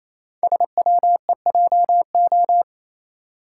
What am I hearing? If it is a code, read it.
Morse code, heard as a keyed tone, about 27 wpm, 700 Hz. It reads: SWEJO
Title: Code Groups - 5 Letters - 28wpm